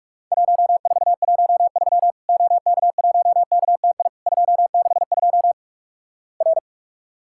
Morse code, heard as a keyed tone, about 48 wpm, 700 Hz. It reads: 1413 OK1KTI 262 R
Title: Worked All Europe DX Contest - QTC Exchange - 45wpm